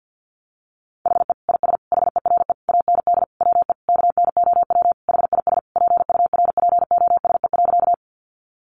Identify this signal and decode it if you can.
Morse code, heard as a keyed tone, about 50 wpm, 700 Hz. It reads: HE IS HERE AND WE KNOW HIS PURPOSE.
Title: Sentences from Top 800 Words 50wpm